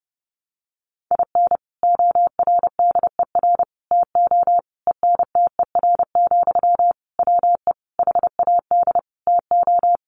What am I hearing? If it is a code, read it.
Morse code, heard as a keyed tone, about 30 wpm, 700 Hz. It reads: IN ORDER TO ENTER, WE HAD TO
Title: Sentences from Top 1300 Words 30wpm